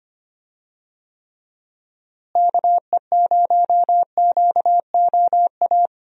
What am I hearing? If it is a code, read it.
Morse code, heard as a keyed tone, about 25 wpm, 700 Hz. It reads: KE0QOA
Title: USA Call Signs - 6 Characters - Repeated 3x - 25wpm